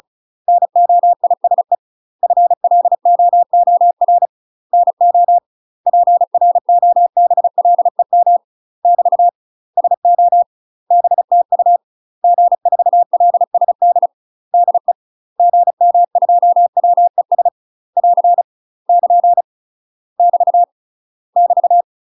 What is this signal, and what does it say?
Morse code, heard as a keyed tone, about 36 wpm, 700 Hz. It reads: NOISE FLOOR NO PROBLEM <BT> SO BTU G4LSD DE GM2WES <AR> <KN> <BT> <BT>
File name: UK MASTER QSOs TRAINING / 35WPM / UK MASTER QSOs 0018